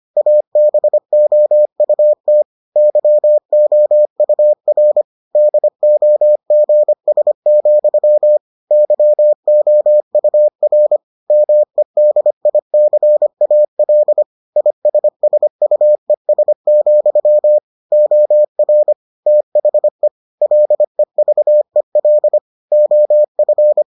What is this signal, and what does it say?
Morse code, heard as a keyed tone, about 25 wpm, 600 Hz. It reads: ABOUT YOUR DOGS, YOUR MEDICAL ISSUES, OR THE LEVEL OF